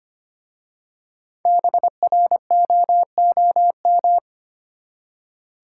Morse code, heard as a keyed tone, about 25 wpm, 700 Hz. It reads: BROOM
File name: practice / 25wpm / Easy Words with Hints - 25wpm